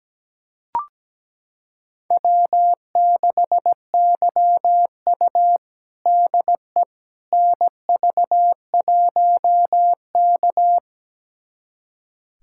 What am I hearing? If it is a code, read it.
Morse code, heard as a keyed tone, about 17 wpm, 700 Hz. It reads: W6YU DE NV1K
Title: USA Callsign DE Callsign - 4 Characters - 17wpm